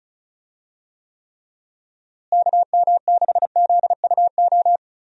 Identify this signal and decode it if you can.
Morse code, heard as a keyed tone, about 34 wpm, 700 Hz. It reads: KM6ZUO
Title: USA Call Signs - 6 Characters - Repeated 2x - 35wpm